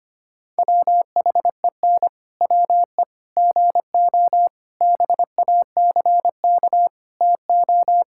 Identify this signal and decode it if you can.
Morse code, heard as a keyed tone, about 25 wpm, 700 Hz. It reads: WHEN WE GO BACK TO
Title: Pangram Challenge v2 - 25wpm